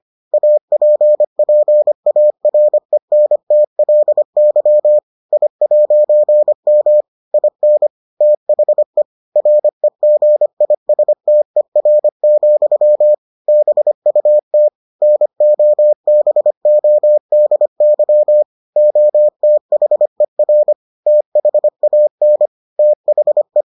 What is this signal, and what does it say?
Morse code, heard as a keyed tone, about 25 wpm, 600 Hz. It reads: APPARENTLY I'M IN THE REGISTER, BUT NOBODY OTHER THAN THE